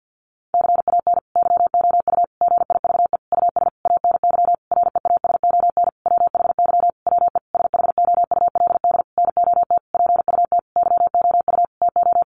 Morse code, heard as a keyed tone, about 50 wpm, 700 Hz. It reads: CAN YOU GIVE US ANY REASON WHY WE SHOULD NOT PUT YOU TO
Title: Sentences from Top 1100 Words 50wpm